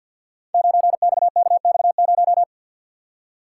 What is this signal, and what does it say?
Morse code, heard as a keyed tone, about 52 wpm, 700 Hz. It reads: 9XKX0
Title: Code Groups - 5 Letter:Numbers - 50wpm